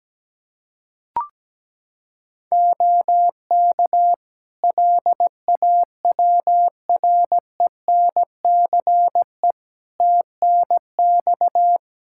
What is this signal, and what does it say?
Morse code, heard as a keyed tone, about 17 wpm, 700 Hz. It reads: OK LAWRENCE TNX
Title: Practice QSO - Exchange 4 - Ending - 17wpm